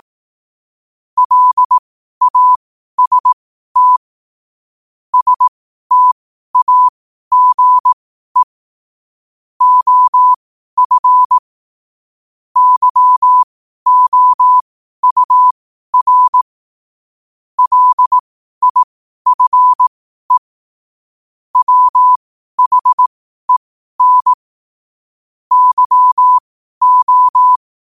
Morse code, heard as a keyed tone, about 18 wpm, spaced as for 12 wpm, 1000 Hz. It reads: LAST STAGE OF YOUR LIFE WHEN YO
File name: qotdfiles / 12wpm